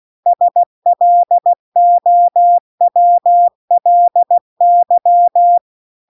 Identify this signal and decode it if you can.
Morse code, heard as a keyed tone, about 16 wpm, 700 Hz. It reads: SLOWLY